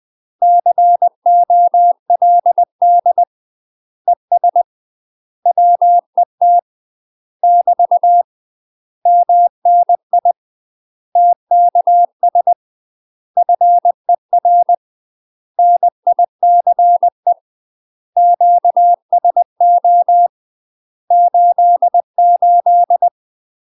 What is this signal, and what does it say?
Morse code, heard as a keyed tone, about 20 wpm, 700 Hz. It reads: COLD ES WET <BT> MNI TKS FER NICE QSO 88
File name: UK MASTER QSOs TRAINING / 20WPM / UK MASTER QSOs 0067